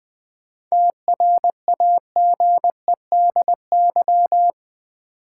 Morse code, heard as a keyed tone, about 20 wpm, 700 Hz. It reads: TRAGEDY
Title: 7 Letter Words - First 3 Letters Given - 20wpm